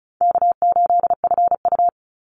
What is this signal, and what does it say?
Morse code, heard as a keyed tone, about 34 wpm, 700 Hz. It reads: K8FU
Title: Easy Call Signs Repeated 2x - 35wpm